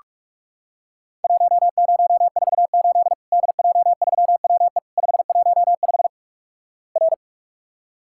Morse code, heard as a keyed tone, about 46 wpm, 700 Hz. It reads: 1048 DJ3WE 515 R
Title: Worked All Europe DX Contest - QTC Exchange - 45wpm